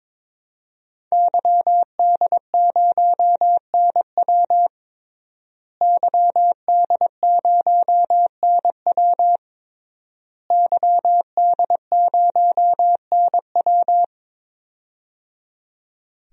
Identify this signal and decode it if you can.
Morse code, heard as a keyed tone, about 22 wpm, 700 Hz. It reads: YD0NW YD0NW YD0NW
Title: DXCC-100 Call Signs - 5 Characters - Repeated 3x - 22wpm